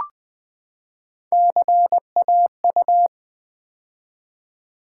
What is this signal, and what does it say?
Morse code, heard as a keyed tone, about 20 wpm, 700 Hz. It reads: CAU
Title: Build A Word - 3 Letters At A Time - 8-Letter Words - 20wpm